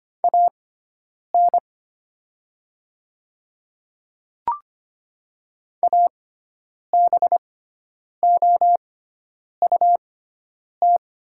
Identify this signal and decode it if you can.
Morse code, heard as a keyed tone, about 25 wpm, spaced as for 9 wpm, 700 Hz. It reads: AN ABOUT
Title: 6x Character Spacing - Top 300 Words - 25wpm